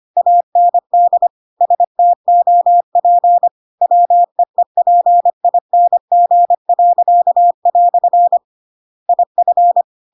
Morse code, heard as a keyed tone, about 25 wpm, 700 Hz. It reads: AND STOP WEEPING." IF
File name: Pinocchio / Pinocchio ch7 25 25 ik1hge